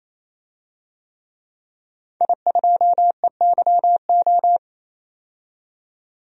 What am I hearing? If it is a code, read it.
Morse code, heard as a keyed tone, about 28 wpm, 700 Hz. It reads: I2EYO
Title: DXCC-100 Call Signs - 5 Characters - 28wpm